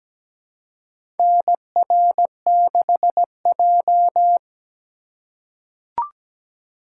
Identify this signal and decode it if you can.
Morse code, heard as a keyed tone, about 17 wpm, 700 Hz. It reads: NR6J
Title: USA Call Signs - 4 Characters - 2x Character Spacing - 17wpm